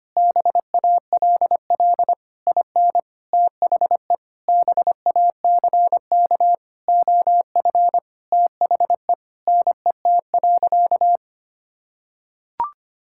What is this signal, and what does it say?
Morse code, heard as a keyed tone, about 25 wpm, 700 Hz. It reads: BALL IN THE BACK OF THE NET.
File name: practice / 25wpm / Sentences with 9-Word Phrases - 25wpm